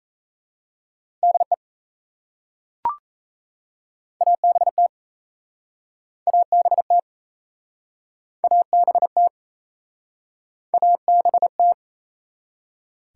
Morse code, heard as a keyed tone, about 38 wpm, 700 Hz. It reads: DE ABT ABT ABT ABT